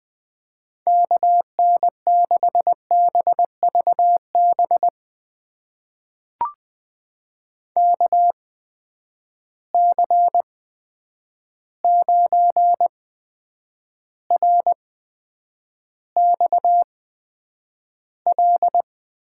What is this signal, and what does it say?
Morse code, heard as a keyed tone, about 20 wpm, spaced as for 9 wpm, 700 Hz. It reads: KN6BVB KC9RXL
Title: USA Call Signs - 6 Characters - 8x Character Spacing - 20wpm